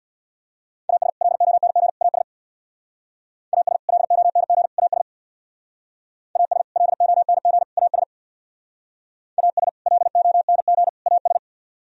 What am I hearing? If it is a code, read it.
Morse code, heard as a keyed tone, about 60 wpm, 700 Hz. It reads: AS LONG AS AS LONG AS AS LONG AS AS LONG AS
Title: Speed-Racing - Sets of 3 Words 50wpm